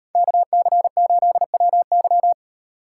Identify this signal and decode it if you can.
Morse code, heard as a keyed tone, about 38 wpm, 700 Hz. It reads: KC8WY